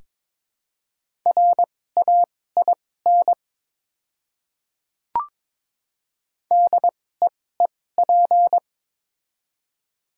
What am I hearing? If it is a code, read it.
Morse code, heard as a keyed tone, about 22 wpm, spaced as for 16 wpm, 700 Hz. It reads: RAIN DEEP
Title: 2x Character Spacing - Top 500 Words - 22wpm